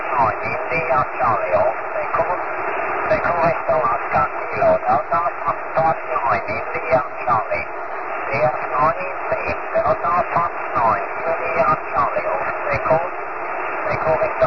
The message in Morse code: none